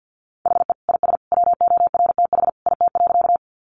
Morse code, heard as a keyed tone, about 50 wpm, 700 Hz. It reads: HE IS WORTH IT.
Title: Sentences from Top 800 Words 50wpm